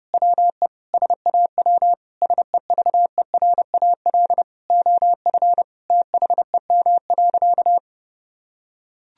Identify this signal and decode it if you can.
Morse code, heard as a keyed tone, about 30 wpm, 700 Hz. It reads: WE SAW SEVERAL OF THEM.
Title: Sentences from Top 1000 Words - Review Part 2 - 30wpm